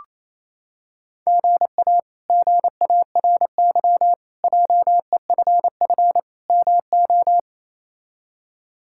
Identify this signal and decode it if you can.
Morse code, heard as a keyed tone, about 28 wpm, 700 Hz. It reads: GA GARY JEFF MO
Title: K1USN SST QSO Practice - 28wpm